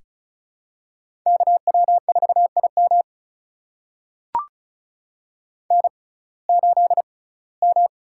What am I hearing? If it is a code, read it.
Morse code, heard as a keyed tone, about 36 wpm, spaced as for 21 wpm, 700 Hz. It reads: KW4IM N8M
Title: USA Call Signs - 5 Characters - 6x Character Spacing - 35wpm